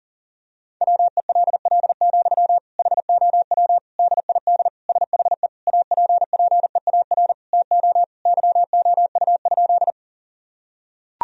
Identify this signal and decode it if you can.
Morse code, heard as a keyed tone, about 40 wpm, 700 Hz. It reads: WELL, HOW DID SHE APPEAR TO YOU?
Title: Sentences from Top 1000 Words - Review Part 2 - 40wpm